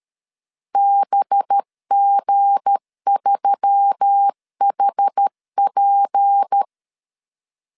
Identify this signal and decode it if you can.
Morse code, heard as a keyed tone, about 13 wpm, 800 Hz. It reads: BG3HP
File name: LEZIONI / 7 GF9Z / 7 G 12wpm 2055